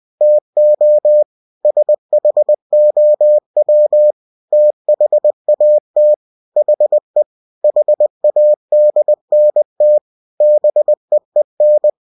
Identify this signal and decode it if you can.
Morse code, heard as a keyed tone, about 20 wpm, 600 Hz. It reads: TO SHOW THAT HE HADNT BEEN